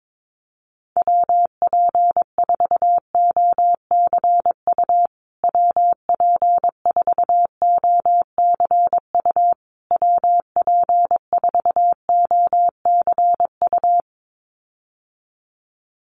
Morse code, heard as a keyed tone, about 22 wpm, 700 Hz. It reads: WP4OCU WP4OCU WP4OCU
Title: USA Call Signs Repeated 3x 22wpm